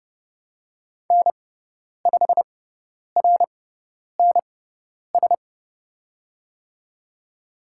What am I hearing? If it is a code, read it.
Morse code, heard as a keyed tone, about 29 wpm, spaced as for 10 wpm, 700 Hz. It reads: N5RNS
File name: Lesson 16g - Letter c - Call Signs 30wpm